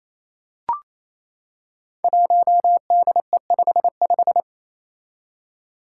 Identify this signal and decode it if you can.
Morse code, heard as a keyed tone, about 27 wpm, 700 Hz. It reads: 1DE55
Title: Five Challenging Letter-Number Combinations 28wpm